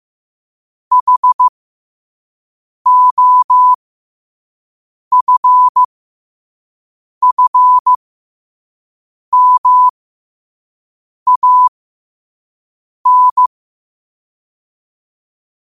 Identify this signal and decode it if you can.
Morse code, heard as a keyed tone, about 15 wpm, spaced as for 5 wpm, 1000 Hz. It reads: HOFFMAN
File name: qotdfiles / 5wpm